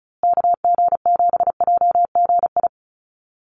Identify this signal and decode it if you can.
Morse code, heard as a keyed tone, about 34 wpm, 700 Hz. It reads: KG7JGI
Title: USA Call Signs Repeated 3x 35wpm